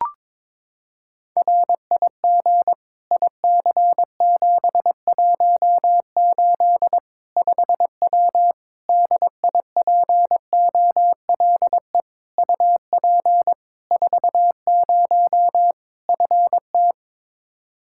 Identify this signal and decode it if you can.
Morse code, heard as a keyed tone, about 22 wpm, 700 Hz. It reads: RIG IC718 5W DIPOLE UP 40 FT
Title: K1USN SST QSO Practice 22wpm